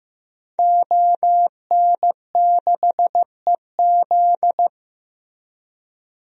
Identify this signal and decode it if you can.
Morse code, heard as a keyed tone, about 15 wpm, 700 Hz. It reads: ON6EZ
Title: DXCC-100 Call Signs - 5 Characters - Repeated 3x - 15wpm